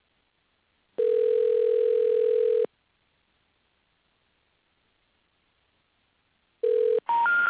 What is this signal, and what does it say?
Ringback tone (North America), 440 + 480 Hz.